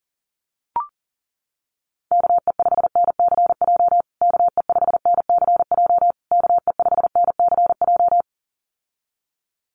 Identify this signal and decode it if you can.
Morse code, heard as a keyed tone, about 37 wpm, 700 Hz. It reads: KE5NCJ KE5NCJ KE5NCJ
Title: USA Call Signs Repeated 3x 40wpm